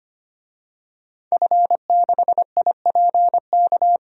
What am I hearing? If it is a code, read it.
Morse code, heard as a keyed tone, about 25 wpm, 700 Hz. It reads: F6IPK
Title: DXCC-100 Call Signs - 5 Characters - Repeated 2x - 25wpm